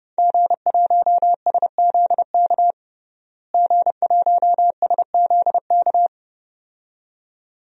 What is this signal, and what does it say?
Morse code, heard as a keyed tone, about 32 wpm, 700 Hz. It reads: G1SZK G1SZK
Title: DXCC-100 Call Signs - 5 Characters - Repeated 2x - 30wpm